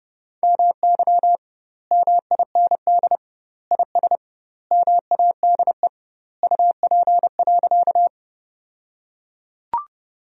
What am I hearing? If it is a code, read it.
Morse code, heard as a keyed tone, about 30 wpm, 700 Hz. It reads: MY MIND IS MADE UP.
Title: 2x Word-Spacing - Sentences from Top 500 Words - 30wpm